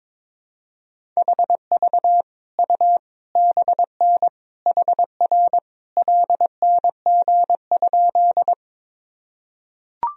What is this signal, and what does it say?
Morse code, heard as a keyed tone, about 22 wpm, 700 Hz. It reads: HV U BN HR LNG?
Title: Sentences from Top 200 Words and Abbreviations 22wpm